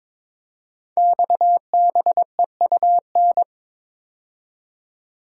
Morse code, heard as a keyed tone, about 22 wpm, 700 Hz. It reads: XBEUN
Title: Code Groups - 5 Letters - 22wpm